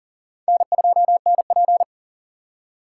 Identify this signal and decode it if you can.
Morse code, heard as a keyed tone, about 39 wpm, 700 Hz. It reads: N2NP
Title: USA Call Signs - 4 Characters - 3x Character Spacing - 40wpm